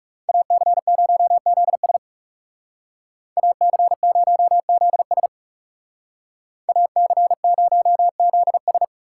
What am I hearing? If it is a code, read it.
Morse code, heard as a keyed tone, about 36 wpm, 700 Hz. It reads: AC0ZS AC0ZS AC0ZS
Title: Speed Racing - USA 2x2 Call Signs 30wpm